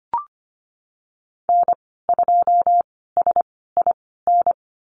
Morse code, heard as a keyed tone, about 25 wpm, spaced as for 16 wpm, 700 Hz. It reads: N2SIN
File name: Lesson 15g - Number 5 - Call Signs 25wpm